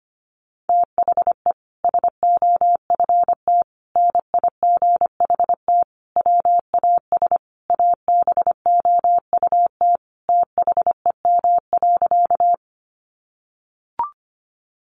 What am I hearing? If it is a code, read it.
Morse code, heard as a keyed tone, about 25 wpm, 700 Hz. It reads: THE SOFT NIGHT WAS ABOUT THEM.
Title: Sentences from Top 1500 Words 25wpm